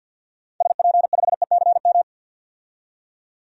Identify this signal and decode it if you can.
Morse code, heard as a keyed tone, about 49 wpm, 700 Hz. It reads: SP5ECM
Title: DXCC-100 Call Signs - 6 Characters - 50wpm